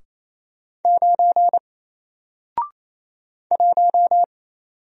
Morse code, heard as a keyed tone, about 27 wpm, 700 Hz. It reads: 9 1
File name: Single-Letter-Number-28wpm